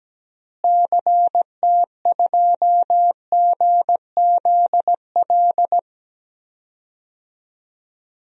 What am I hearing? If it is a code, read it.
Morse code, heard as a keyed tone, about 17 wpm, 700 Hz. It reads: CT2GZL
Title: DXCC-100 Call Signs - 6 Characters - Repeated 2x - 17wpm